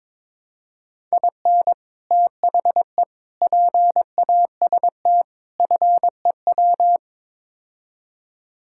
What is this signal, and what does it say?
Morse code, heard as a keyed tone, about 22 wpm, 700 Hz. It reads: IN THE PAST FEW
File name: practice / 22wpm / Build A Phrase - 5-Word Phrases - 22wpm